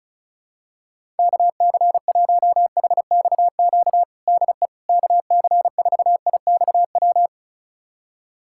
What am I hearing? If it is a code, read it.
Morse code, heard as a keyed tone, about 36 wpm, 700 Hz. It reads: KC1HXQ DE KC4IXW
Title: USA Callsign DE Callsign - 6 Characters - 35wpm